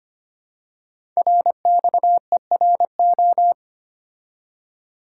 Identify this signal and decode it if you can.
Morse code, heard as a keyed tone, about 25 wpm, 700 Hz. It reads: RXERO
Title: Code Groups - 5 Letters - 25wpm